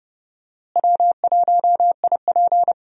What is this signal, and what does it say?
Morse code, heard as a keyed tone, about 30 wpm, 700 Hz. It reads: W1IP
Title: USA Call Signs - 4 Characters - Repeated 2x - 30wpm